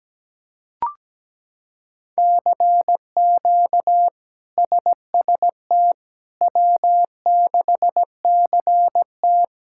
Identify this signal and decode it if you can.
Morse code, heard as a keyed tone, about 17 wpm, 700 Hz. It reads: CQ SST W6CT
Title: K1USN SST QSO Practice - 17wpm